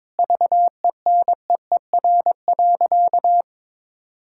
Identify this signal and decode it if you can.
Morse code, heard as a keyed tone, about 22 wpm, 700 Hz. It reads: VENEER.
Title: Pangram Challenge v2 - 22wpm